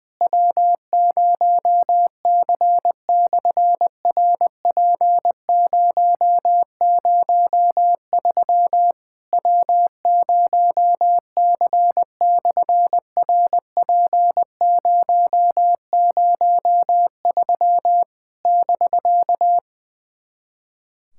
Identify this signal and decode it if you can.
Morse code, heard as a keyed tone, about 20 wpm, 700 Hz. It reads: W0C/RP003 W0C/RP003 <BK>
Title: SOTA QSO Practice 20wpm